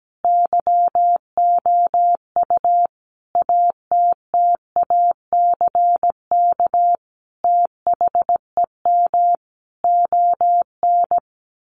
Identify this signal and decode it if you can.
Morse code, heard as a keyed tone, about 17 wpm, 700 Hz. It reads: YOU ATTACK THEM ON
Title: Sentences from Top 1500 Words 17wpm